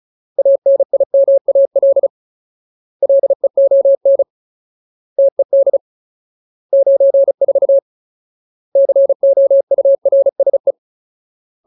Morse code, heard as a keyed tone, about 34 wpm, 550 Hz. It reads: ANIMAL LEON TED 94 COURSE
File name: Ultimate Word List / 35WPM Ultimate